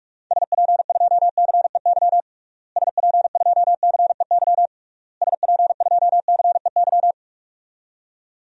Morse code, heard as a keyed tone, about 46 wpm, 700 Hz. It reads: SP2CEY SP2CEY SP2CEY
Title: DXCC-100 Call Signs - 6 Characters - Repeated 3x - 45wpm